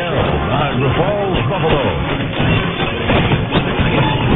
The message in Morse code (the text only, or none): none